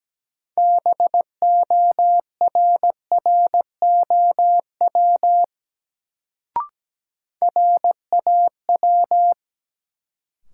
Morse code, heard as a keyed tone, about 17 wpm, 700 Hz. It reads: BORROW RAW
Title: Sentences from Top 1500 Words - New Word Drill - 17wpm